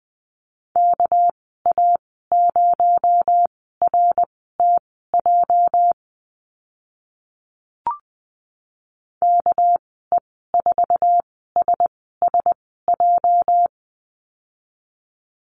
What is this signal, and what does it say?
Morse code, heard as a keyed tone, about 20 wpm, spaced as for 15 wpm, 700 Hz. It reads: KA0RTJ KE4SSJ
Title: 245 Lesson 39g - Letter z - Call Signs 20wpm